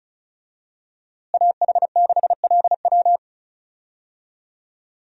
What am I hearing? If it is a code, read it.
Morse code, heard as a keyed tone, about 34 wpm, 700 Hz. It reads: AH6LW
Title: USA Call Signs - 5 Characters - Repeated 2x - 35wpm